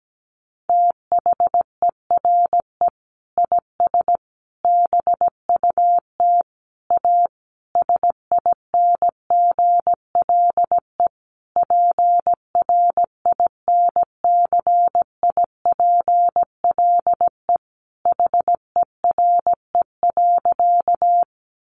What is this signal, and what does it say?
Morse code, heard as a keyed tone, about 17 wpm, 700 Hz. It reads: THERE IS BUT A SINGLE PRINCIPLE HERE.
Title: Sentences from Top 1400 Words 17wpm